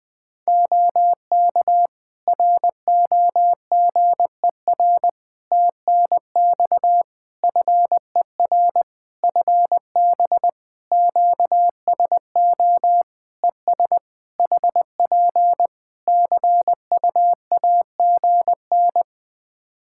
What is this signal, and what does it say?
Morse code, heard as a keyed tone, about 20 wpm, 700 Hz. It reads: OK ROGER TNX FER FB QSO ES HP CUAGN
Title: Practice QSO - Exchange 4 - Ending - 20wpm